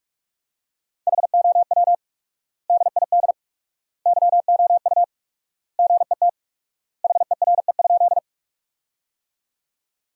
Morse code, heard as a keyed tone, about 46 wpm, 700 Hz. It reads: HOW DID YOU GET HERE?
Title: 4x Word-Spacing - Sentences from Top 200 Words - 45wpm